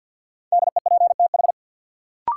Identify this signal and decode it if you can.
Morse code, heard as a keyed tone, about 46 wpm, 700 Hz. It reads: DEPTH
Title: Sentences from Top 1200 Words - New Word Drill - 50wpm